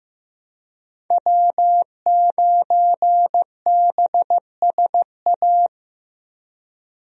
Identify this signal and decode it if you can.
Morse code, heard as a keyed tone, about 15 wpm, 700 Hz. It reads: W9BSA
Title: USA Call Signs - 5 Characters - Repeated 2x - 15wpm